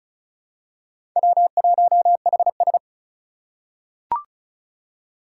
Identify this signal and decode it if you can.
Morse code, heard as a keyed tone, about 36 wpm, 700 Hz. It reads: W1HS
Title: USA Call Signs - 3x Character Spacing - 35wpm